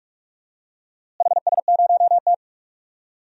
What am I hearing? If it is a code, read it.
Morse code, heard as a keyed tone, about 46 wpm, 700 Hz. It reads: HS0T
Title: DXCC-100 Call Signs - 4 Characters - Repeated 3x - 45wpm